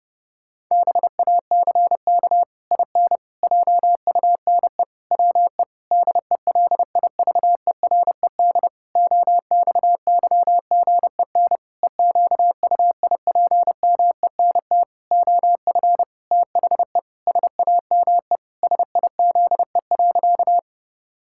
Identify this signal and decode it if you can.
Morse code, heard as a keyed tone, about 30 wpm, 700 Hz. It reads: BACK IN JUNE WE DELIVERED OXYGEN EQUIPMENT OF THE SAME SIZE.
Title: Pangram Challenge v2 - 30wpm